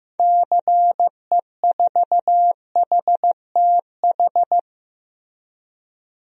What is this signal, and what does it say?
Morse code, heard as a keyed tone, about 15 wpm, 700 Hz. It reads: CE4HTH
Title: DXCC-100 Call Signs - 6 Characters - Repeated 2x - 15wpm